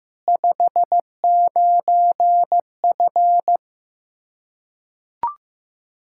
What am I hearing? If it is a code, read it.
Morse code, heard as a keyed tone, about 15 wpm, 700 Hz. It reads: 59F
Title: K1USN SST QSO Practice 15wpm